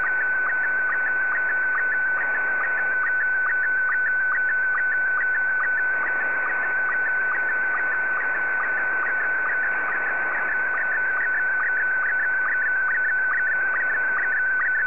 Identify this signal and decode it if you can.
SSTV (slow-scan television), Scottie 1.